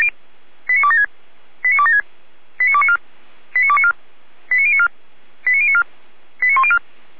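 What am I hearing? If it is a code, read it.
Five-tone selective call (ZVEI), heard as quick runs of five tones, tones of about 70 ms. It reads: 89276 89276 89294 89294 89094 89094 89104
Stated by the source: ZVEI five-tone selective call 89265, received by SDR.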